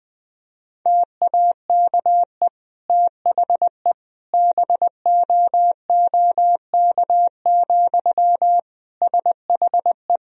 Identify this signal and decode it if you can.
Morse code, heard as a keyed tone, about 20 wpm, 700 Hz. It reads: TAKE THE BOOK, SHE
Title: Sentences from Top 1000 Words - Review Part 1 - 20wpm